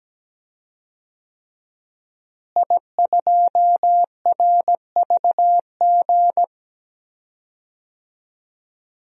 Morse code, heard as a keyed tone, about 17 wpm, 700 Hz. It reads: I2RVG
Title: DXCC-100 Call Signs - 5 Characters - Repeated 3x - 17wpm